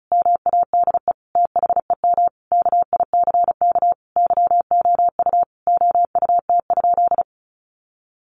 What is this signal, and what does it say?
Morse code, heard as a keyed tone, about 34 wpm, 700 Hz. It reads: MADE THEM KICK YOU OUT?
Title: Sentences from Top 1300 Words 35wpm